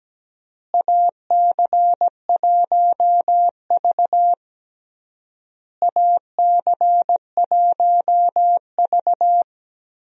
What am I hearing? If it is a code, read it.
Morse code, heard as a keyed tone, about 17 wpm, 700 Hz. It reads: AC1V AC1V
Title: USA Call Signs - 4 Characters - Repeated 3x - 17wpm